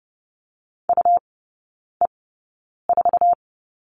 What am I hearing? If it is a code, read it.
Morse code, heard as a keyed tone, about 30 wpm, spaced as for 9 wpm, 690 Hz. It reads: UE4
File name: Lesson 17e - Letter u - Three Characters 30wpm